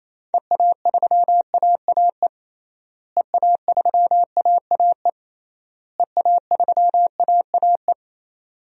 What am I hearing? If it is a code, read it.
Morse code, heard as a keyed tone, about 28 wpm, 700 Hz. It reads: EA3AAE EA3AAE EA3AAE
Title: DXCC-100 Call Signs - 6 Characters - Repeated 3x - 28wpm